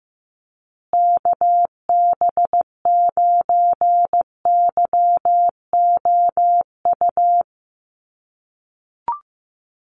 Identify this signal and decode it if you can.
Morse code, heard as a keyed tone, about 14 wpm, 700 Hz. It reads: KB9YOU
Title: USA Call Signs Repeated 2x 15wpm